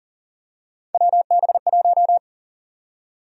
Morse code, heard as a keyed tone, about 40 wpm, 700 Hz. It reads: WB1
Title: USA Call Sign Prefixes - 40wpm